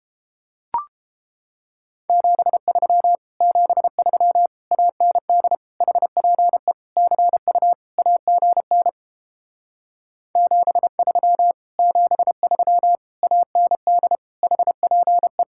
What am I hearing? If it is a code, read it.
Morse code, heard as a keyed tone, about 29 wpm, 700 Hz. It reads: 73 73 AND HPE CU AGN 73 73 AND HPE